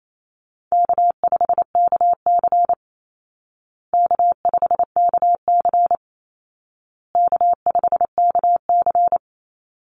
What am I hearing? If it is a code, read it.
Morse code, heard as a keyed tone, about 28 wpm, 700 Hz. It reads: K5KC K5KC K5KC
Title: Easy Call Signs Repeated 3x - 28wpm